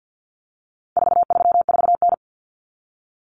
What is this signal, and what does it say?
Morse code, heard as a keyed tone, about 50 wpm, 700 Hz. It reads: 434N
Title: Common Cut Numbers - 4-Digit 50wpm